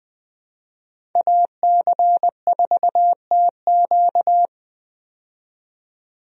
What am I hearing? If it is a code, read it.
Morse code, heard as a keyed tone, about 21 wpm, 700 Hz. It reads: AC4TQ
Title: USA Call Signs - 5 Characters - 20wpm